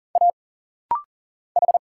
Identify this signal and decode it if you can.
Morse code, heard as a keyed tone, about 40 wpm, 700 Hz. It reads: A H
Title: Single Letters - Rapid-Fire 40wpm